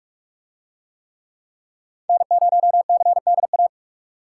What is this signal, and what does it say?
Morse code, heard as a keyed tone, about 44 wpm, 700 Hz. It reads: N0CDA